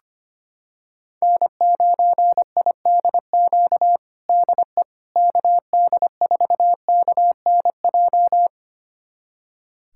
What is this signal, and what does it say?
Morse code, heard as a keyed tone, about 25 wpm, 700 Hz. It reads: N9IDQ DE KD4KNJ
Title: USA Callsign DE Callsign 25wpm